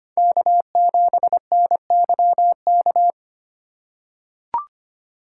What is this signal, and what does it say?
Morse code, heard as a keyed tone, about 25 wpm, 700 Hz. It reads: K7NYK
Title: USA Call Signs - 5 Characters - 4x Character Spacing - 25wpm